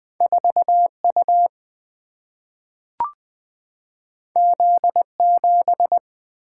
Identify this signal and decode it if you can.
Morse code, heard as a keyed tone, about 20 wpm, 700 Hz. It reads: 4U Z7
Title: Single-Letter Single-Number 20wpm